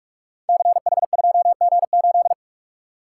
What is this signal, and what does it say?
Morse code, heard as a keyed tone, about 44 wpm, 700 Hz. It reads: CH2G8
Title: Code Groups - 5 Letter:Numbers - 45wpm